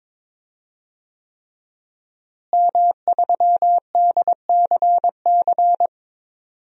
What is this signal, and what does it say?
Morse code, heard as a keyed tone, about 22 wpm, 700 Hz. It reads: M3DCC
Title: DXCC-100 Call Signs - 5 Characters - Repeated 3x - 22wpm